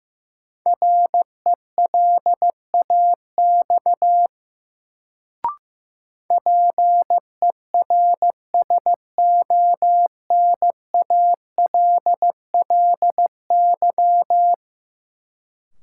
Morse code, heard as a keyed tone, about 16 wpm, 700 Hz. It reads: RELAX PERSONALLY
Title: Sentences from Top 1400 Words - New Word Drill Alt - 15wpm